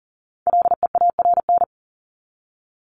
Morse code, heard as a keyed tone, about 40 wpm, 700 Hz. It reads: LEARN
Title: ING Suffix - Top 100 Words with Base Words 40wpm